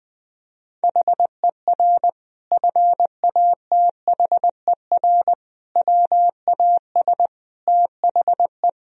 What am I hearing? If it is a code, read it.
Morse code, heard as a keyed tone, about 20 wpm, 700 Hz. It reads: HER FATHER WAS THE
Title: Sentences from Top 1000 Words - Review Part 4 - 20wpm